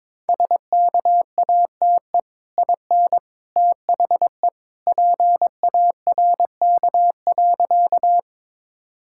Morse code, heard as a keyed tone, about 22 wpm, 700 Hz. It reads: SKATE IN THE PARK.
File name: practice / 22wpm / Easy Sentences with Hints - 22wpm